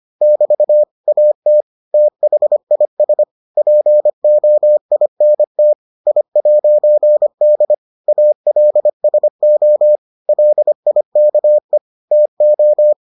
Morse code, heard as a keyed tone, about 25 wpm, 600 Hz. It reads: <BT> AT THIS POINT I'D ALSO LIKE TO